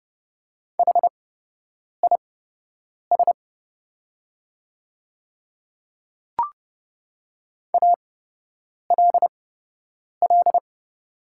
Morse code, heard as a keyed tone, about 30 wpm, spaced as for 8 wpm, 700 Hz. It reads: HIS ALL
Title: 8x Character Spacing - Top 100 Words - 30wpm